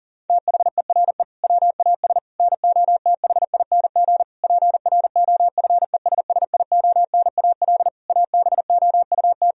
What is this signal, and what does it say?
Morse code, heard as a keyed tone, about 40 wpm, 700 Hz. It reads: THERE WAS NOTHING PROFESSIONAL ABOUT
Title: Sentences from Top 1000 Words - Review Part 4 - 40wpm